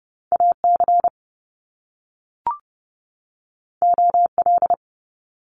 Morse code, heard as a keyed tone, about 30 wpm, 700 Hz. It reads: AC OL
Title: Top Two Letter Combinations in Words 30wpm